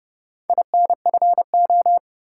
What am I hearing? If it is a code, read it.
Morse code, heard as a keyed tone, about 30 wpm, 700 Hz. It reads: INFO
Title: Top 180 QSO Elements 30wpm